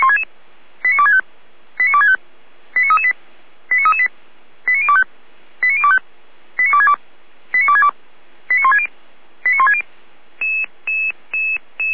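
Five-tone selective call (ZVEI), heard as quick runs of five tones, tones of about 70 ms. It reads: 89265 89265 89338 89338 89025 89025 89262 89262 89160 89160
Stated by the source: ZVEI five-tone selective call 89099 received by SDR, from the Feuerwehr Orlamünde audio archive.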